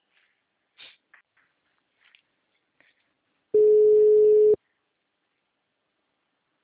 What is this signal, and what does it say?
Ringback tone (Europe, 425 Hz), with speech.